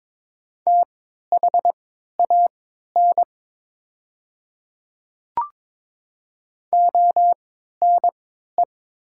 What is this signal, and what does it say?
Morse code, heard as a keyed tone, about 22 wpm, spaced as for 12 wpm, 690 Hz. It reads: THAN ONE
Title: 3x Character Spacing - Top 100 Words - 22wpm